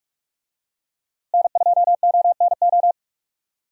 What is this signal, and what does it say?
Morse code, heard as a keyed tone, about 46 wpm, 700 Hz. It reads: N2ONO
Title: USA Call Signs - 5 Characters - Repeated 3x - 45wpm